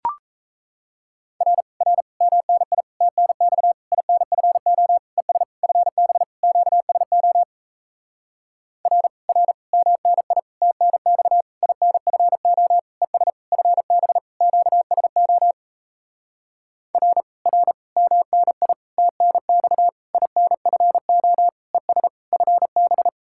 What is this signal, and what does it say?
Morse code, heard as a keyed tone, about 37 wpm, 700 Hz. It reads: R R MNI TNX INFO ES FB QSO R R MNI TNX INFO ES FB QSO R R MNI TNX INFO ES FB